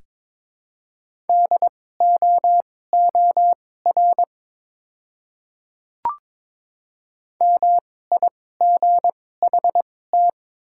Morse code, heard as a keyed tone, about 23 wpm, spaced as for 16 wpm, 700 Hz. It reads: DOOR MIGHT
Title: 2x Character Spacing - Top 300 Words - 22wpm